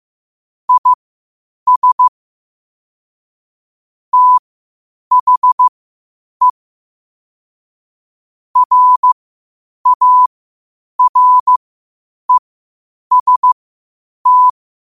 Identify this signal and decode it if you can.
Morse code, heard as a keyed tone, about 15 wpm, spaced as for 8 wpm, 1000 Hz. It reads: IS THE RAREST
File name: qotdfiles / 8wpm